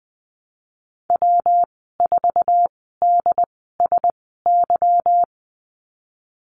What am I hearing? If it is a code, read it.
Morse code, heard as a keyed tone, about 20 wpm, spaced as for 15 wpm, 700 Hz. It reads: W4DSY
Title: 167 Lesson 27g - Number 7 - Call Signs 20wpm